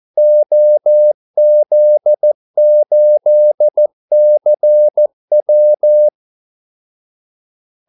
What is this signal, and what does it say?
Morse code, heard as a keyed tone, about 14 wpm, 600 Hz. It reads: OZ8CW
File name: Callsigns-14wpm